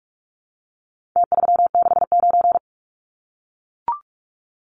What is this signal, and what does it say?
Morse code, heard as a keyed tone, about 44 wpm, 700 Hz. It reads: T369
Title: Numbers - Contest Style - 4-Digit 45wpm